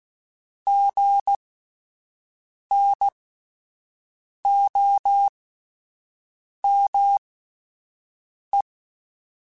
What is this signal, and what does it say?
Morse code, heard as a keyed tone, about 16 wpm, spaced as for 5 wpm, 800 Hz. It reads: GNOME
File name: Spaced Alphas / Spaced Alphas mp3s / Gs sp @10